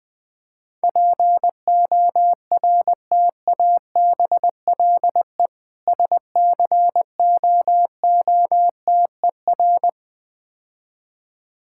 Morse code, heard as a keyed tone, about 20 wpm, 700 Hz. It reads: PORTABLE SCOOTER
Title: Difficult Words - Speed-Racing with Hints - 20wpm